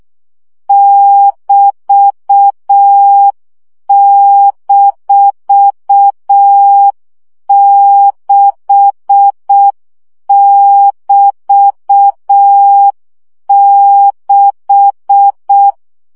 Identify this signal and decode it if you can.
Morse code, heard as a keyed tone, about 6 wpm, 800 Hz. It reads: <BT>-6<BT>6
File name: CW gyakorlas b 6 egyenlo minusz 5 baud